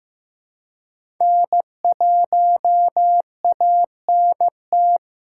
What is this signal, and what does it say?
Morse code, heard as a keyed tone, about 16 wpm, 700 Hz. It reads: N1ANT
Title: USA Call Signs - 5 Characters - 4x Character Spacing - 15wpm